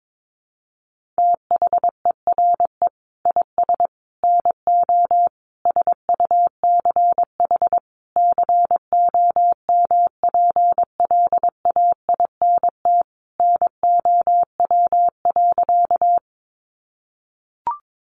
Morse code, heard as a keyed tone, about 22 wpm, 700 Hz. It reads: THERE IS NO SUCH COMPLAINT NOW.
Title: Sentences from Top 1400 Words 22wpm